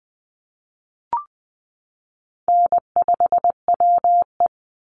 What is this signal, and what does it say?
Morse code, heard as a keyed tone, about 20 wpm, 700 Hz. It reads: N5WE
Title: Call Signs Repeated 2x - 20wpm